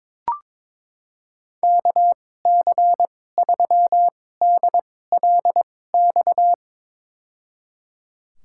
Morse code, heard as a keyed tone, about 22 wpm, spaced as for 16 wpm, 700 Hz. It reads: KC3DLX
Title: USA Call Signs - 6 Characters - 2x Character Spacing - 22wpm